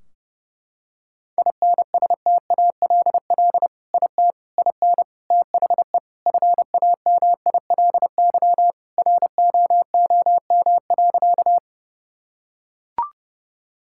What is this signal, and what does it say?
Morse code, heard as a keyed tone, about 30 wpm, 700 Hz. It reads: INSTALL IT IN THE FAMILY ROOM.
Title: Sentences from Top 1600 Words 30wpm